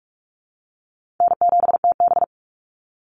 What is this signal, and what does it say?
Morse code, heard as a keyed tone, about 44 wpm, 700 Hz. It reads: N7TB
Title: Easy Call Signs Repeated 2x - 45wpm